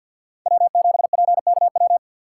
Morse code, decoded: W7PKW

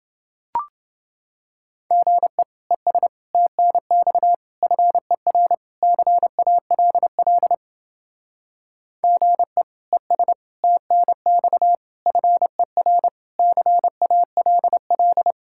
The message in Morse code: GE ES TNX FER CALL GE ES TNX FER CALL

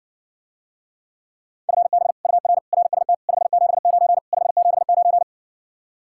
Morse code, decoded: FB UR RST 579 579

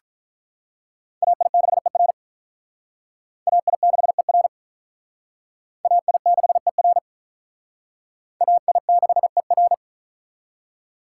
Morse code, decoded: AI6ER AI6ER AI6ER AI6ER

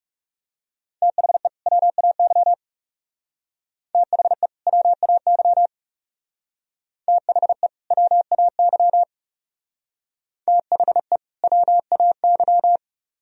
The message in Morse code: THE WAY THE WAY THE WAY THE WAY

